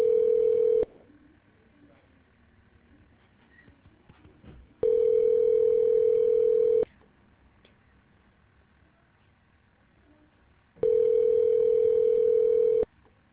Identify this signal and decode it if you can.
Ringback tone (North America), 440 + 480 Hz.